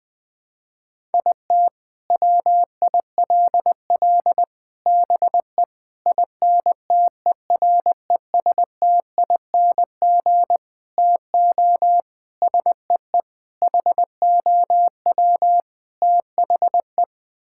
Morse code, heard as a keyed tone, about 21 wpm, 700 Hz. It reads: IT WILL BE INTERESTING TO SEE HOW THE
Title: Sets of 8 Words - 20wpm